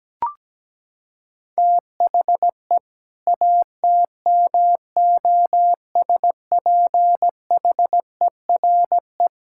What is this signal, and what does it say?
Morse code, heard as a keyed tone, about 17 wpm, 700 Hz. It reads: THE ATMOSPHERE
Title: Sentences from Top 1600 Words 17wpm